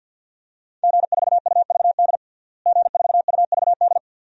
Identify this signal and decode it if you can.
Morse code, heard as a keyed tone, about 52 wpm, 700 Hz. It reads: G4UVD G4UVD
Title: DXCC-100 Call Signs - 5 Characters - Repeated 2x - 50wpm